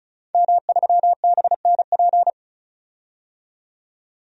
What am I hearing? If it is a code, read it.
Morse code, heard as a keyed tone, about 36 wpm, 700 Hz. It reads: M3BNP
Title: DXCC-100 Call Signs - 5 Characters - 35wpm